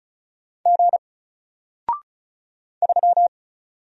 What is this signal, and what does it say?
Morse code, heard as a keyed tone, about 35 wpm, 700 Hz. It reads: G 3
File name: Single-Letter-Number-35wpm